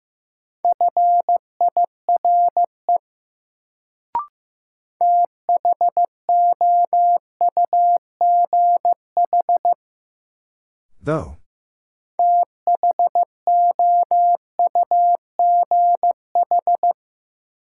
Morse code, heard as a keyed tone, about 15 wpm, 700 Hz. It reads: FIRE THOUGH THOUGH
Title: Sentences from Top 400 Words - New Word Drill - 15wpm